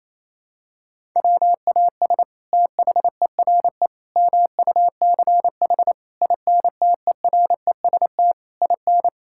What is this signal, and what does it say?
Morse code, heard as a keyed tone, about 28 wpm, 700 Hz. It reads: WAS THERE MUCH INTEREST IN
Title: Sentences from Top 1000 Words - Review Part 5 - 28wpm